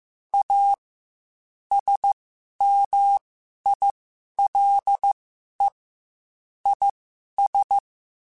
Morse code, heard as a keyed tone, about 15 wpm, spaced as for 11 wpm, 800 Hz. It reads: A SMILE IS
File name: Quotes / Quotes-mp3s / A Smile @13